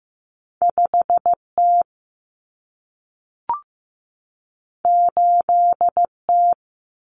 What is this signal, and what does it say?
Morse code, heard as a keyed tone, about 15 wpm, 700 Hz. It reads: 5T 8T
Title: Common Cut Numbers - 2-Digit 15wpm